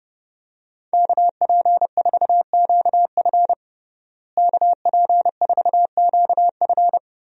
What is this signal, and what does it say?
Morse code, heard as a keyed tone, about 32 wpm, 700 Hz. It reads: KP4QF KP4QF
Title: DXCC-100 Call Signs - 5 Characters - Repeated 2x - 30wpm